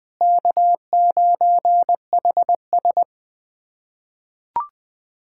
Morse code, heard as a keyed tone, about 20 wpm, 700 Hz. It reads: K9HS
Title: USA Callsign DE Callsign 20wpm